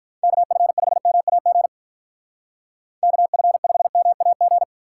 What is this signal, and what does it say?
Morse code, heard as a keyed tone, about 47 wpm, 700 Hz. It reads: KF5MAG KF5MAG